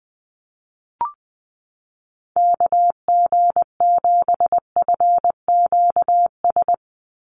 Morse code, heard as a keyed tone, about 20 wpm, 700 Hz. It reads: KG7FQS
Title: USA Call Signs 20wpm